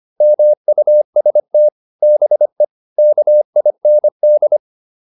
Morse code, heard as a keyed tone, about 25 wpm, 600 Hz. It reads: MUST BE KIND